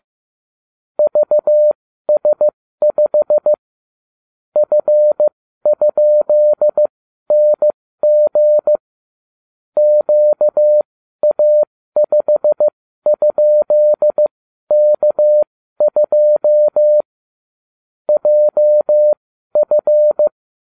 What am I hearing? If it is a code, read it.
Morse code, heard as a keyed tone, about 15 wpm, spaced as for 12 wpm, 600 Hz. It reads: VS5 F?NG QA5?K2 JF